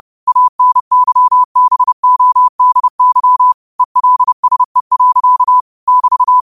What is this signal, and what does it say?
Morse code, heard as a keyed tone, about 30 wpm, 1000 Hz. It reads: ANYBODY ELSE. <BT>
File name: qotdfiles / 30wpm